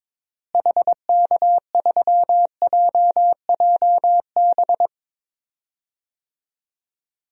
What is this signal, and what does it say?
Morse code, heard as a keyed tone, about 22 wpm, 700 Hz. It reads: HK3JJB
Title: DXCC-100 Call Signs - 6 Characters - Repeated 2x - 22wpm